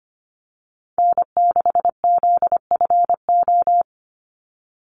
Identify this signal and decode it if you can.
Morse code, heard as a keyed tone, about 25 wpm, 700 Hz. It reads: N6ZFO
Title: Call Signs Repeated 3x - 25wpm